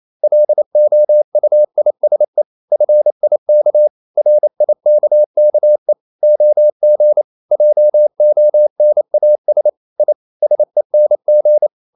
Morse code, heard as a keyed tone, about 28 wpm, 600 Hz. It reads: LOUISE FIK RIKKE OG JONAS I SENG
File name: Ferie QRQ 28wpm